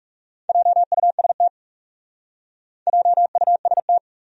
Morse code, heard as a keyed tone, about 36 wpm, 700 Hz. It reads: JUST JUST
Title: Speed-Racing Top 300 Words 30wpm